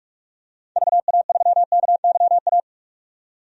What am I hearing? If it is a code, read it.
Morse code, heard as a keyed tone, about 43 wpm, 700 Hz. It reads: VA3KYA
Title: DXCC-100 Call Signs - 6 Characters - 45wpm